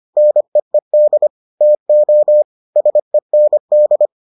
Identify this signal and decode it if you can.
Morse code, heard as a keyed tone, about 25 wpm, 600 Hz. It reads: NEED TO SEND